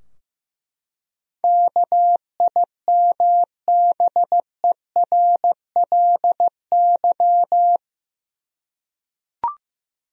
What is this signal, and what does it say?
Morse code, heard as a keyed tone, about 15 wpm, 700 Hz. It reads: KIMBERLY
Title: USA Common Names - 15wpm